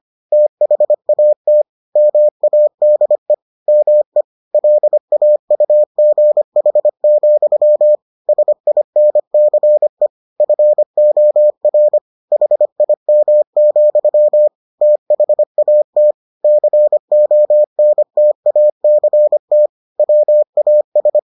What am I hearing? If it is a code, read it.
Morse code, heard as a keyed tone, about 25 wpm, 600 Hz. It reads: THAT MADE ME LAUGH, SINCE FOR HIM, THAT CONTACT WAS